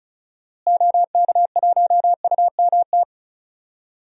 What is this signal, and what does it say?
Morse code, heard as a keyed tone, about 34 wpm, 700 Hz. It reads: OK1UMT
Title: DXCC-100 Call Signs - 6 Characters - Repeated 3x - 35wpm